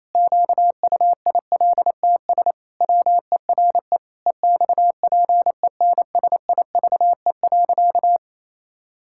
Morse code, heard as a keyed tone, about 27 wpm, 700 Hz. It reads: QUILTS WERE EXPENSIVE.
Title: Pangram Challenge v2 - 28wpm